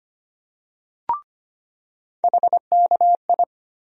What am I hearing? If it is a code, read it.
Morse code, heard as a keyed tone, about 25 wpm, 700 Hz. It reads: HKI
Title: USA Call Sign Postfixes - 3 Letters 25wpm